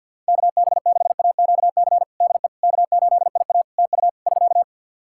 Morse code, heard as a keyed tone, about 52 wpm, 700 Hz. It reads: KB6AQC DE K8IA TU <SK>
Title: K1USN SST QSO Practice 50wpm